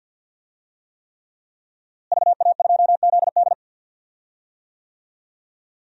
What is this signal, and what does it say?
Morse code, heard as a keyed tone, about 52 wpm, 700 Hz. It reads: VA2ZD
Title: DXCC-100 Call Signs - 5 Characters - Repeated 3x - 50wpm